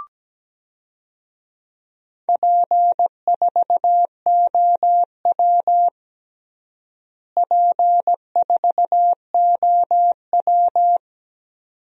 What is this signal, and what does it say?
Morse code, heard as a keyed tone, about 17 wpm, 700 Hz. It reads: P4OW P4OW
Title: DXCC-100 Call Signs - 4 Characters - Repeated 2x - 17wpm